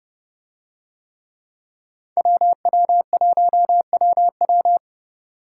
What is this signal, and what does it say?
Morse code, heard as a keyed tone, about 29 wpm, 700 Hz. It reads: WW1WW